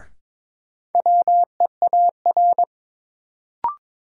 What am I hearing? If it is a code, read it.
Morse code, heard as a keyed tone, about 22 wpm, 700 Hz. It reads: WEAR
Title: Sentences from Top 1000 Words - Review Part 6 - New Word Drill - 22wpm